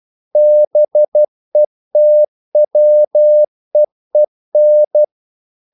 Morse code, heard as a keyed tone, about 12 wpm, 600 Hz. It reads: BETWEEN
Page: English quotations in CW (telegraphy - morse)